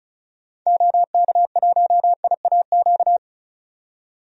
Morse code, heard as a keyed tone, about 36 wpm, 700 Hz. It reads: OK1IAQ
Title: DXCC-100 Call Signs - 6 Characters - 35wpm